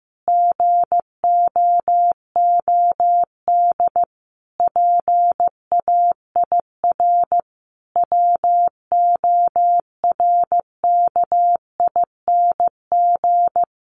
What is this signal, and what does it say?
Morse code, heard as a keyed tone, about 15 wpm, 700 Hz. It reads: GOOD PAIR WORKING
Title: Sentences from Top 1200 Words 15wpm